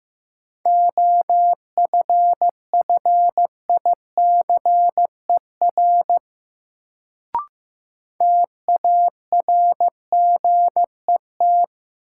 Morse code, heard as a keyed tone, about 15 wpm, 700 Hz. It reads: OFFICER TARGET
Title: Sentences from Top 1200 Words - New Word Drill - 15wpm